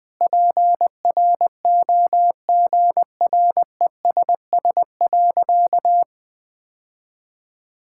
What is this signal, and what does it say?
Morse code, heard as a keyed tone, about 21 wpm, 700 Hz. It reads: PROGRESS.
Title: Sentences with 6-Word Phrases - 20wpm